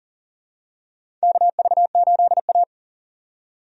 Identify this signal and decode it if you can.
Morse code, heard as a keyed tone, about 40 wpm, 700 Hz. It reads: KV8A
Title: USA Call Signs - 4 Characters - 40wpm